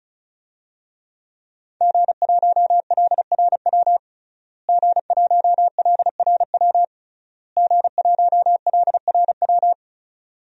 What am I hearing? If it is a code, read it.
Morse code, heard as a keyed tone, about 36 wpm, 700 Hz. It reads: G1LRW G1LRW G1LRW
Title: DXCC-100 Call Signs - 5 Characters - Repeated 3x - 35wpm